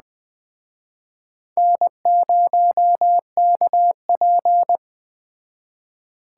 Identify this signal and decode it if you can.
Morse code, heard as a keyed tone, about 20 wpm, 700 Hz. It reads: N0KP